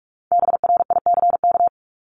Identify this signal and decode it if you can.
Morse code, heard as a keyed tone, about 44 wpm, 700 Hz. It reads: BRICK
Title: Top 1751-2000 Words 45wpm